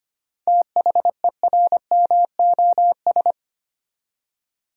Morse code, heard as a keyed tone, about 25 wpm, 690 Hz. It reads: THERMOS
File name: practice / 25wpm / Easy Words with Hints - 25wpm